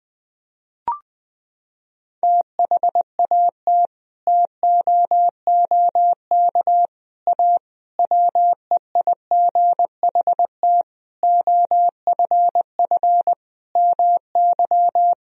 Morse code, heard as a keyed tone, about 21 wpm, 700 Hz. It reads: THAT TOOK A WEIGHT OFF MY
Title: Sentences from Top 1000 Words - Review Part 3 - 20wpm